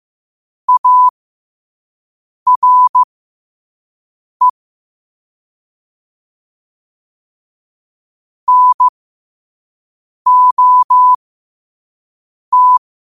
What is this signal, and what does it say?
Morse code, heard as a keyed tone, about 15 wpm, spaced as for 5 wpm, 1000 Hz. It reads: ARE NOT
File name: qotdfiles / 5wpm